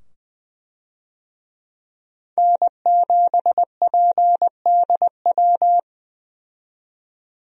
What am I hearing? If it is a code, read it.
Morse code, heard as a keyed tone, about 20 wpm, 700 Hz. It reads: N7PDW
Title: USA Call Signs - 5 Characters - 20wpm